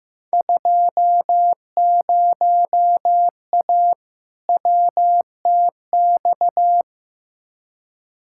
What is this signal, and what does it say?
Morse code, heard as a keyed tone, about 15 wpm, 700 Hz. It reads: 20A WTX
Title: Field Day Exchanges Encore - 15wpm